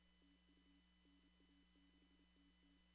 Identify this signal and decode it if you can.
Morse code, heard as a keyed tone, about 28 wpm, 280 Hz, with noise. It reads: TU NA6TT
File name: CW / kt5j wpx cw-op-k5pi